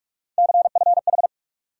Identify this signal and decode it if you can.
Morse code, heard as a keyed tone, about 44 wpm, 700 Hz. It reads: CFH
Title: USA Call Sign Postfixes - 3 Letters 45wpm